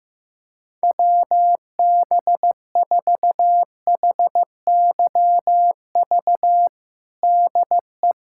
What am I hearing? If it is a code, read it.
Morse code, heard as a keyed tone, about 15 wpm, 700 Hz. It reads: WB4HYV DE
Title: USA Callsign DE Callsign - 6 Characters - 15wpm